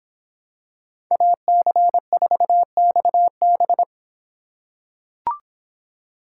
Morse code, heard as a keyed tone, about 26 wpm, 700 Hz. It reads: AC4XB